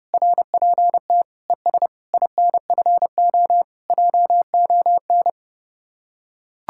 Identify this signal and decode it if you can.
Morse code, heard as a keyed tone, about 30 wpm, 700 Hz. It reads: RPT ES INFO JON